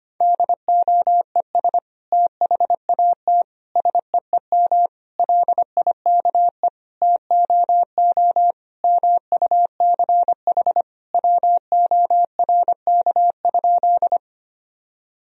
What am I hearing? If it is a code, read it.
Morse code, heard as a keyed tone, about 25 wpm, 700 Hz. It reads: DOES THAT SEEM LIKE TOO MUCH WORK?
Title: Sentences from Top 1000 Words - Review Part 1 - 25wpm